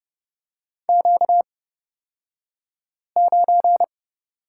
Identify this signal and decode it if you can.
Morse code, heard as a keyed tone, about 30 wpm, spaced as for 5 wpm, 700 Hz. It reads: Q9
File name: Lesson 38d - Letter q - Two Characters 30wpm